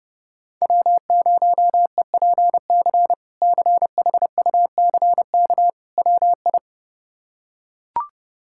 Morse code, heard as a keyed tone, about 30 wpm, 700 Hz. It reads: W0EPC CHUCK WI
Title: K1USN SST QSO Practice - 30wpm